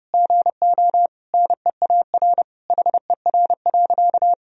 Morse code, heard as a keyed tone, about 32 wpm, 700 Hz. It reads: GO NEAR HER.
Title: Sentences from Top 1000 Words - Review Part 4 - 30wpm